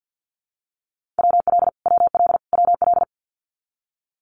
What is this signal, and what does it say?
Morse code, heard as a keyed tone, about 49 wpm, 700 Hz. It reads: WL WL WL
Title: Abbreviations Repeated 3x - 50wpm